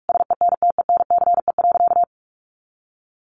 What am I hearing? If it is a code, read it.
Morse code, heard as a keyed tone, about 44 wpm, 700 Hz. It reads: SENTENCE.
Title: Sentences from Top 1200 Words 45wpm